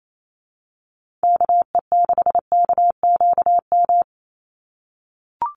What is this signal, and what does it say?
Morse code, heard as a keyed tone, about 28 wpm, 700 Hz. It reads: KE6KQM
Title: USA Call Signs 28wpm